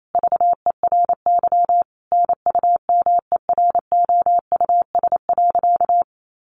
Morse code, heard as a keyed tone, about 28 wpm, 700 Hz. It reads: VERY NUMEROUS.
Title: Sentences from Top 1400 Words 28wpm